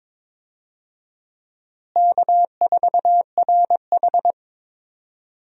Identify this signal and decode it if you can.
Morse code, heard as a keyed tone, about 22 wpm, 700 Hz. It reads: K4RH
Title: USA Call Signs - 4 Characters - Repeated 2x - 22wpm